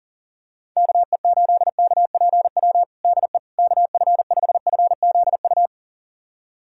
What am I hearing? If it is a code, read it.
Morse code, heard as a keyed tone, about 40 wpm, 700 Hz. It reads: KE8KPW DE KF5FZU
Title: USA Callsign DE Callsign 40wpm